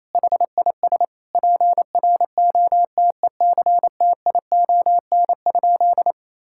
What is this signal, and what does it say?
Morse code, heard as a keyed tone, about 28 wpm, 700 Hz. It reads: HIS PROTECTION?
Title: Sentences from Top 1600 Words 28wpm